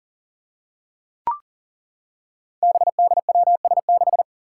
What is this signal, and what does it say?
Morse code, heard as a keyed tone, about 38 wpm, 700 Hz. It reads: BDWS6
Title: Five Challenging Letter-Number Combinations 40wpm